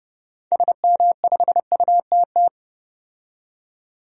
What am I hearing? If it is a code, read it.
Morse code, heard as a keyed tone, about 32 wpm, 700 Hz. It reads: SM5UTT
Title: DXCC-100 Call Signs - 6 Characters - 30wpm